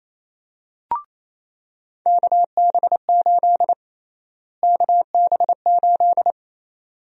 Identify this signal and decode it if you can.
Morse code, heard as a keyed tone, about 28 wpm, 700 Hz. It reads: KB8 KB8
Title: USA Call Sign Prefixes - Repeated 2x - 28wpm